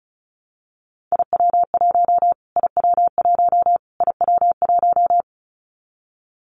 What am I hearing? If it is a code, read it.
Morse code, heard as a keyed tone, about 34 wpm, 700 Hz. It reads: IW1 IW1 IW1
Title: Callsign Prefixes for Top 25 DXCC - Repeated 3x - 35wpm